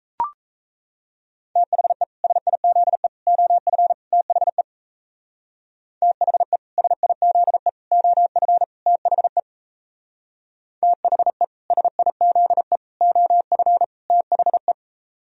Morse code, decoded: THE SIZE OF THE THE SIZE OF THE THE SIZE OF THE

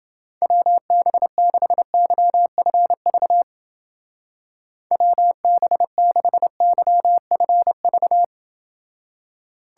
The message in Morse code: WB6YFV WB6YFV

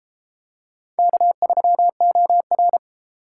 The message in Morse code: K3OR

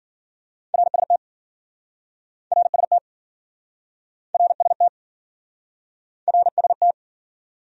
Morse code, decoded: RST RST RST RST

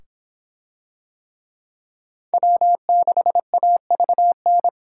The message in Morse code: W6AVN

